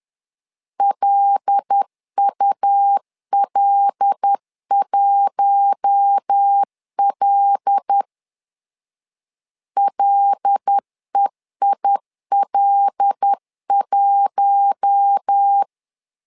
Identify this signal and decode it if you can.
Morse code, heard as a keyed tone, about 10 wpm, 800 Hz. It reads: LUL1L LEIL1
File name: LEZIONI / 5 LBVC / 5 L 10wpm 2259